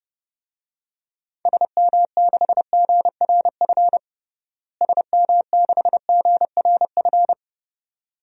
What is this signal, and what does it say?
Morse code, heard as a keyed tone, about 32 wpm, 700 Hz. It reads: SM6GRF SM6GRF
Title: DXCC-100 Call Signs - 6 Characters - Repeated 2x - 30wpm